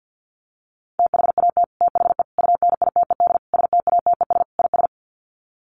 Morse code, heard as a keyed tone, about 50 wpm, 700 Hz. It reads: THAT THE UNITED STATES IS
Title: Sets of 5 Words Encore 50wpm